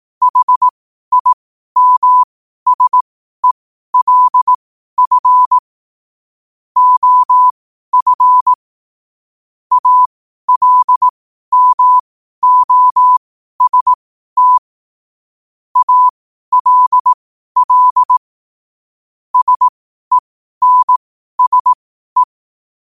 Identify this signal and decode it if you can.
Morse code, heard as a keyed tone, about 18 wpm, spaced as for 12 wpm, 1000 Hz. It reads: HIMSELF OF ALMOST ALL SENSE
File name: qotdfiles / 12wpm